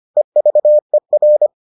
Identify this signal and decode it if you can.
Morse code, heard as a keyed tone, about 25 wpm, 600 Hz. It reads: EVER